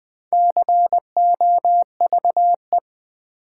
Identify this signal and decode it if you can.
Morse code, heard as a keyed tone, about 20 wpm, 700 Hz. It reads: COVE
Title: Build A Word - 1 Letter At A Time - 3 to 5-Letter Words - 20wpm